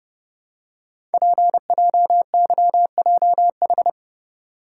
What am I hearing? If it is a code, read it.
Morse code, heard as a keyed tone, about 30 wpm, 690 Hz. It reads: PJYJH
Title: Code Groups - 5 Letters - 30wpm